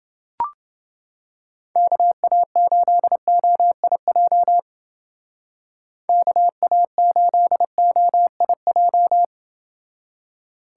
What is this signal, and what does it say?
Morse code, heard as a keyed tone, about 28 wpm, 700 Hz. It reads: KA8OIJ KA8OIJ